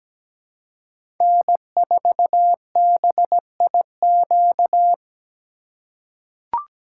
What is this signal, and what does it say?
Morse code, heard as a keyed tone, about 17 wpm, 700 Hz. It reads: N4BIQ
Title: USA Call Signs - 5 Characters - 3x Character Spacing - 17wpm